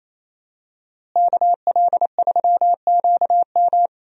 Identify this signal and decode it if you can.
Morse code, heard as a keyed tone, about 28 wpm, 700 Hz. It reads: KL3QM
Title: USA Call Signs - 5 Characters - Repeated 2x - 28wpm